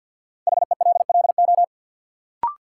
Morse code, heard as a keyed tone, about 50 wpm, 700 Hz. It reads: HELLO
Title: Sentences from Top 1100 Words - New Word Drill - 50wpm